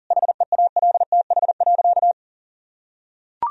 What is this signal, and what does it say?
Morse code, heard as a keyed tone, about 40 wpm, 700 Hz. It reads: HEALTH.
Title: Sentences from Top 1000 Words - Review Part 3 - 40wpm